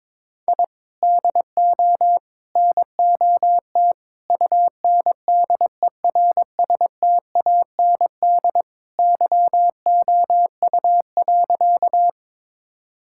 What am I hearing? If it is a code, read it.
Morse code, heard as a keyed tone, about 22 wpm, 700 Hz. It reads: I DO NOT UNDERSTAND YOU.
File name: practice / 22wpm / Sentences with 4-Word Phrases - 22wpm